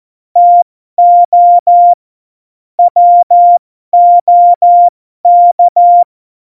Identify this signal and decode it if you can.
Morse code, heard as a keyed tone, about 14 wpm, spaced as for 12 wpm, 700 Hz. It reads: TO WOK
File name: Morsekurs / Witze